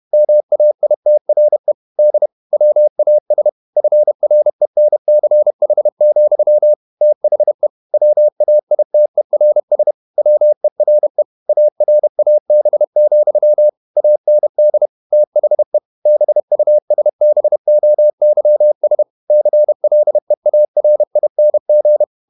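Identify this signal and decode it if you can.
Morse code, heard as a keyed tone, about 31 wpm, 600 Hz. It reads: MAITRE D WAS FRENCH, THE WAITERS WERE ARAB, AND THE BUSBOYS CLEARING
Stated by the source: Morse code at 30 wpm, the novel Never by Ken Follett in English, from CW med Gnister.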